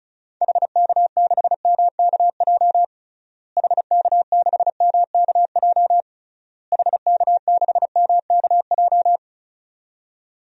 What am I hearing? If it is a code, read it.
Morse code, heard as a keyed tone, about 36 wpm, 700 Hz. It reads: HK6MKJ HK6MKJ HK6MKJ
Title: DXCC-100 Call Signs - 6 Characters - Repeated 3x - 35wpm